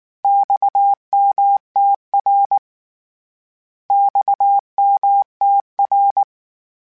Morse code, heard as a keyed tone, about 19 wpm, 800 Hz. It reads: XMTR XMTR
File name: Abreviations / Abreviations mp3s / Abreviations@18